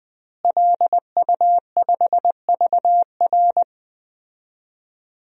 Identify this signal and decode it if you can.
Morse code, heard as a keyed tone, about 20 wpm, 700 Hz. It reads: LU5VR
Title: DXCC-100 Call Signs - 5 Characters - Repeated 3x - 20wpm